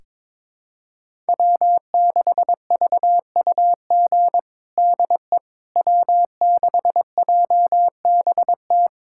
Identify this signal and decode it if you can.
Morse code, heard as a keyed tone, about 22 wpm, 700 Hz. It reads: W6VUG DE W6JBT
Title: USA Callsign DE Callsign - 5 Characters - 22wpm